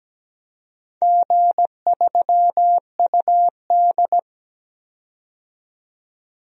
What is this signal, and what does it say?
Morse code, heard as a keyed tone, about 17 wpm, 700 Hz. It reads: G3UD
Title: DXCC-100 Call Signs - 4 Characters - 17wpm